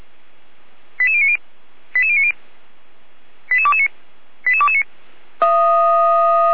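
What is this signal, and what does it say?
Five-tone selective call (ZVEI), heard as quick runs of five tones, tones of about 70 ms. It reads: 88090 88090 88229 88229
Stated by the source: ZVEI five-tone call received by SDR, from the audio server of feuerwehr-orlamuende.org (fire brigade Orlamünde).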